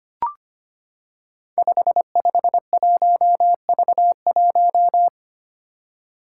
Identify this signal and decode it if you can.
Morse code, heard as a keyed tone, about 25 wpm, 700 Hz. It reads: 551V1
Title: Five Challenging Letter-Number Combinations 25wpm